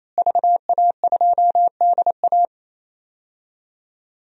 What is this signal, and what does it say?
Morse code, heard as a keyed tone, about 28 wpm, 700 Hz. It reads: VA2DA
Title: DXCC-100 Call Signs - 5 Characters - Repeated 3x - 28wpm